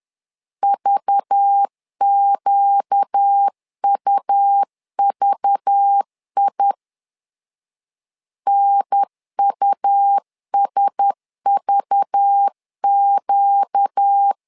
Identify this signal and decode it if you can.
Morse code, heard as a keyed tone, about 10 wpm, 800 Hz. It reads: VQUVI NUSVQ
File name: LEZIONI / 5 LBVC / 5 LBV 10wpm 2248